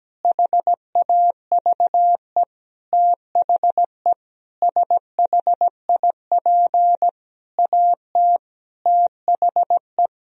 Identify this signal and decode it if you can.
Morse code, heard as a keyed tone, about 17 wpm, 700 Hz. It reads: HAVE THE SHIP AT THE
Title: Sentences from Top 1000 Words - Review Part 2 - 17wpm